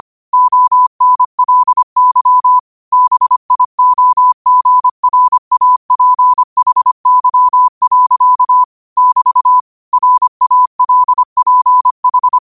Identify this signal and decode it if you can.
Morse code, heard as a keyed tone, about 25 wpm, 1000 Hz. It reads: ONLY BIOGRAPHY. <BT> RALPH
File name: qotdfiles / 25wpm